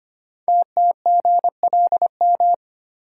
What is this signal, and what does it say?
Morse code, heard as a keyed tone, about 25 wpm, 690 Hz. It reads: TTGLM
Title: Code Groups - 5 Letters - 25wpm